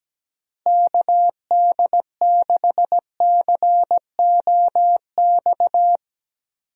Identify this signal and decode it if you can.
Morse code, heard as a keyed tone, about 17 wpm, 700 Hz. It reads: KD6COX
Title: USA Call Signs - 6 Characters - Repeated 2x - 17wpm